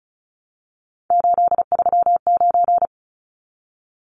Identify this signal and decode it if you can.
Morse code, heard as a keyed tone, about 34 wpm, 700 Hz. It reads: 839
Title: Numbers - 3-Digit 35wpm